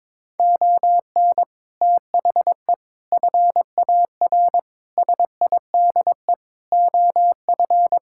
Morse code, heard as a keyed tone, about 22 wpm, 700 Hz. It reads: ON THE FAR SIDE OF
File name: practice / 22wpm / Sets of 6 Words - 22wpm